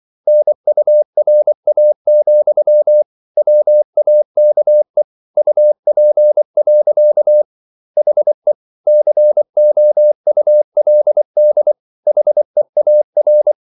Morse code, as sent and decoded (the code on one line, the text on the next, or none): -. ..- .-. .- --..-- / .-- .- -.- . / ..- .--. .-.-.- / .... / -.-. --- ..- .-.. -.. / .... .- .-.
NURA, WAKE UP. HE COULD HEAR